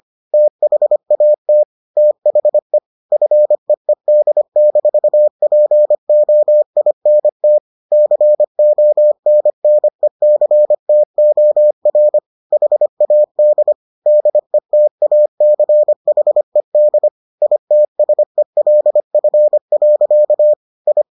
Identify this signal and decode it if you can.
Morse code, heard as a keyed tone, about 25 wpm, 600 Hz. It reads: THAT THE FEED-POINT CONNECTOR HAD DETACHED ITSELF. I